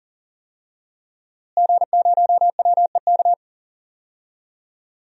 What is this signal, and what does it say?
Morse code, heard as a keyed tone, about 40 wpm, 700 Hz. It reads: G0WEK
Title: DXCC-100 Call Signs - 5 Characters - Repeated 2x - 40wpm